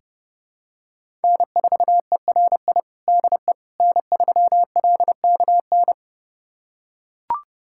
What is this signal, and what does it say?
Morse code, heard as a keyed tone, about 30 wpm, 700 Hz. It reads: N4ERI DE N3LKN
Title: USA Callsign DE Callsign - 5 Characters - 30wpm